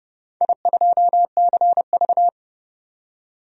Morse code, heard as a keyed tone, about 30 wpm, 700 Hz. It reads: I2CV
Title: DXCC-100 Call Signs - 4 Characters - 30wpm